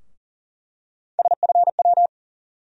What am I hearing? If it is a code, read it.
Morse code, heard as a keyed tone, about 40 wpm, 700 Hz. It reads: SFW
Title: USA Call Sign Postfixes - 3 Letters 40wpm